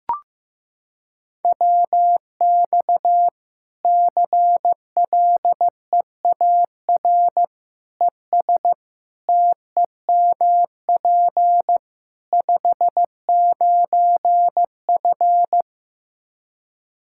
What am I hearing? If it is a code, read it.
Morse code, heard as a keyed tone, about 15 wpm, 700 Hz. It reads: WX CLEAR ES TEMP 59F
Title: K1USN SST QSO Practice 15wpm